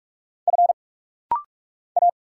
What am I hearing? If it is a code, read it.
Morse code, heard as a keyed tone, about 46 wpm, 700 Hz. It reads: F A
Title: Single Letter-Number - Rapid-Fire 45wpm